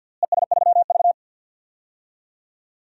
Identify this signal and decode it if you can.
Morse code, heard as a keyed tone, about 53 wpm, 700 Hz. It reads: ES3V
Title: DXCC-100 Call Signs - 4 Characters - Repeated 2x - 50wpm